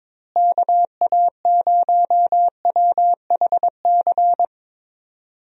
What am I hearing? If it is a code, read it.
Morse code, heard as a keyed tone, about 22 wpm, 700 Hz. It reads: KA0WHC
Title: USA Call Signs - 6 Characters - Repeated 2x - 22wpm